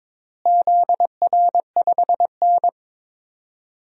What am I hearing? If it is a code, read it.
Morse code, heard as a keyed tone, about 22 wpm, 700 Hz. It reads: ZR5N
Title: DXCC-100 Call Signs - 4 Characters - Repeated 3x - 22wpm